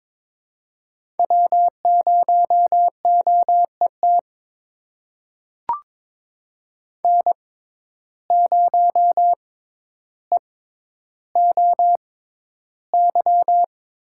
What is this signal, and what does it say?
Morse code, heard as a keyed tone, about 22 wpm, spaced as for 11 wpm, 700 Hz. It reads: W0OET N0EOY
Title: USA Call Signs - 5 Characters - 6x Character Spacing - 22wpm